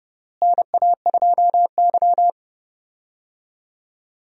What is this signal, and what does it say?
Morse code, heard as a keyed tone, about 31 wpm, 700 Hz. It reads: NA2Y
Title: USA Call Signs - 4 Characters - Repeated 3x - 30wpm